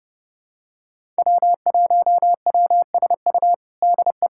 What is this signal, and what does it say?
Morse code, heard as a keyed tone, about 30 wpm, 700 Hz. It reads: W1WSU DE
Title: USA Callsign DE Callsign - 5 Characters - 30wpm